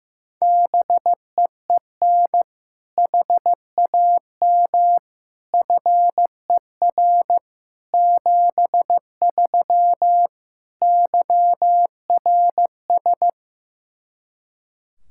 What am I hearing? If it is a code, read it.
Morse code, heard as a keyed tone, about 15 wpm, 700 Hz. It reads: BEEN HAM FER 73 YRS
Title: Practice QSO - Exchange 3 - More Info - 15wpm